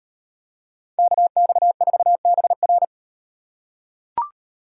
Morse code, heard as a keyed tone, about 38 wpm, 700 Hz. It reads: KX4BR